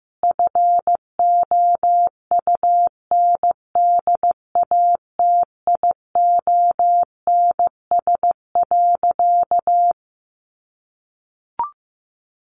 Morse code, heard as a keyed tone, about 15 wpm, 700 Hz. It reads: FOUNDATIONS.
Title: A Sound Of Thunder 15wpm